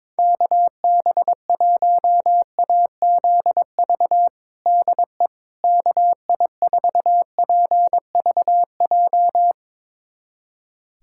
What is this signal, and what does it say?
Morse code, heard as a keyed tone, about 22 wpm, 700 Hz. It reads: KB1AZV DE KI4PVJ
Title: USA Callsign DE Callsign - 6 Characters - 22wpm